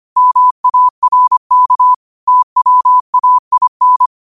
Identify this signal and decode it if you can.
Morse code, heard as a keyed tone, about 25 wpm, 1000 Hz. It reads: MARK TWAIN